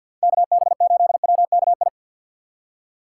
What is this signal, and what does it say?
Morse code, heard as a keyed tone, about 47 wpm, 700 Hz. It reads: KB8WKI
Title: USA Call Signs - 6 Characters - 3x Character Spacing - 50wpm